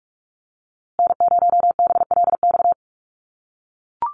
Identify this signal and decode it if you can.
Morse code, heard as a keyed tone, about 44 wpm, 700 Hz. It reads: N0BLX
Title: USA Call Signs Repeated 2x 45wpm